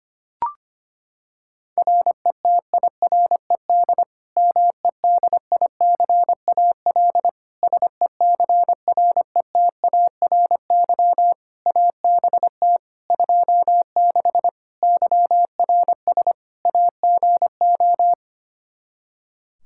Morse code, heard as a keyed tone, about 25 wpm, 700 Hz. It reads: RETIRED MEDICAL SECRETARY ABT 26 YRS AGO
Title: K1USN SST QSO Practice 25wpm